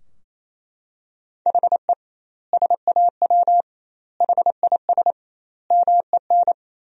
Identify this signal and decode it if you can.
Morse code, heard as a keyed tone, about 28 wpm, 700 Hz. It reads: HE SAW HIS MEN
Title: 2x Word-Spacing - Sentences from Top 300 Words - 28wpm